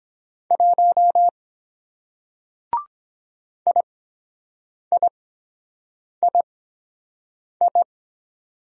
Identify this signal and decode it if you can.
Morse code, heard as a keyed tone, about 23 wpm, spaced as for 7 wpm, 700 Hz. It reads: 1 IIII